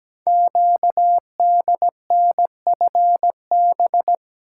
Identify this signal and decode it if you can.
Morse code, heard as a keyed tone, about 17 wpm, 700 Hz. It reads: QDNFB
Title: Code Groups - 5 Letters - 17wpm